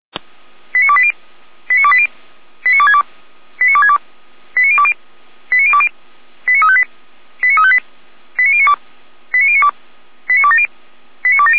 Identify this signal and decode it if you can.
Five-tone selective call (ZVEI), heard as quick runs of five tones, tones of about 70 ms. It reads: 89280 89280 89262 89262 89020 89020 89358 89358 89092 89092 89270 89270